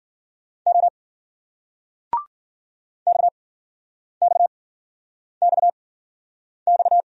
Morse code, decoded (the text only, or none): O X X X X